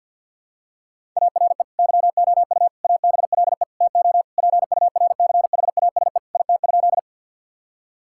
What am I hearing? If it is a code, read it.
Morse code, heard as a keyed tone, about 50 wpm, 700 Hz. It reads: ARE YOU ABLE TO PURCHASE IT?
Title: Sentences from Top 1000 Words - Review Part 5 - 50wpm